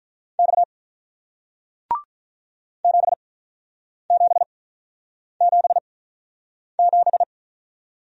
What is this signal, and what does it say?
Morse code, heard as a keyed tone, about 48 wpm, spaced as for 9 wpm, 700 Hz. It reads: X 7777